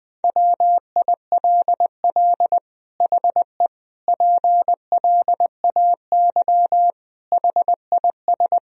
Morse code, heard as a keyed tone, about 20 wpm, 700 Hz. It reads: WILL HE PLAY HIS
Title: Sentences from Top 1000 Words - Review Part 1 - 20wpm